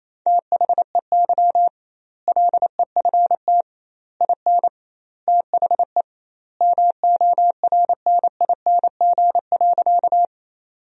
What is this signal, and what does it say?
Morse code, heard as a keyed tone, about 28 wpm, 700 Hz. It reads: THEY LEFT IN THE MORNING.
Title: 2x Word-Spacing - Sentences from Top 400 Words - 28wpm